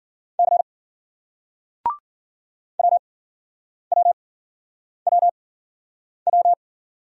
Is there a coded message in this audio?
Yes: Morse code from 0.4 to 6.5 s, heard as a keyed tone, about 50 wpm, spaced as for 9 wpm, 700 Hz. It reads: C WWWW